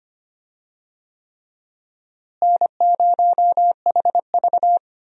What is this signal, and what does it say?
Morse code, heard as a keyed tone, about 25 wpm, 700 Hz. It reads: N0HV
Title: USA Call Signs - 4 Characters - Repeated 3x - 25wpm